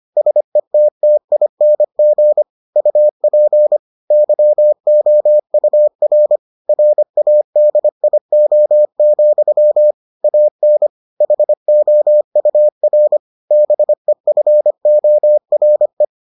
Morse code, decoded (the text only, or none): SETTING UP YOUR RADIO, AN HOUR BEFORE